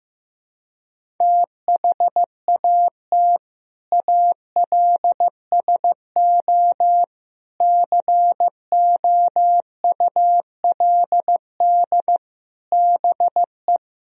Morse code, heard as a keyed tone, about 16 wpm, 700 Hz. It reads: THAT ALSO COULD BE
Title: Sentences from Top 1000 Words - Review Part 1 - 15wpm